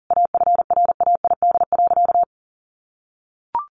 Morse code, heard as a keyed tone, about 37 wpm, 700 Hz. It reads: AFRAID.
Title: Sentences from Top 1500 Words 40wpm